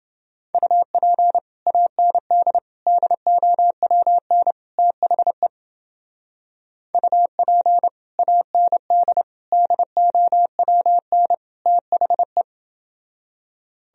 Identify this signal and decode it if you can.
Morse code, heard as a keyed tone, about 29 wpm, 700 Hz. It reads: UP AND DOWN THE UP AND DOWN THE